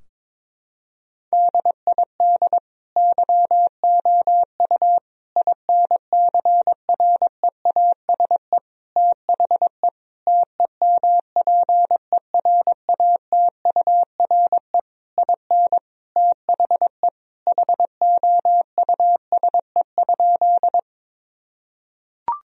Morse code, heard as a keyed tone, about 22 wpm, 700 Hz. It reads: DID YOU INCREASE THE TEMPERATURE IN THE HOUSE?
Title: Sentences from Top 1000 Words - Review Part 4 - 22wpm